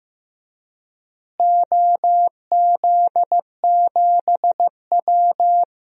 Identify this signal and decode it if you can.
Morse code, heard as a keyed tone, about 15 wpm, 700 Hz. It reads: OZ7W
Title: DXCC-100 Call Signs - 4 Characters - Repeated 2x - 15wpm